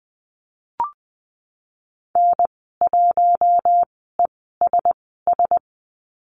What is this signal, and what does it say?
Morse code, heard as a keyed tone, about 20 wpm, spaced as for 15 wpm, 700 Hz. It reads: N1ESS
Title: 076 Lesson 13g - Letter L - Call Signs 20wpm